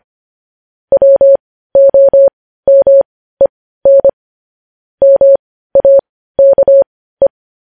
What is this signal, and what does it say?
Morse code, heard as a keyed tone, about 25 wpm, spaced as for 15 wpm, 550 Hz. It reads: WOMEN MAKE
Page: news headlines in morse code at 15 wpm